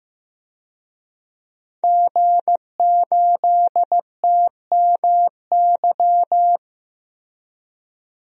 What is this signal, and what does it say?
Morse code, heard as a keyed tone, about 15 wpm, 700 Hz. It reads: G8TMY
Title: DXCC-100 Call Signs - 5 Characters - 15wpm